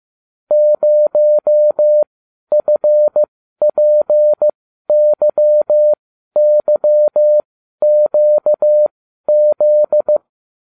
Morse code, heard as a keyed tone, about 15 wpm, spaced as for 12 wpm, 600 Hz. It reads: 0FPYYQZ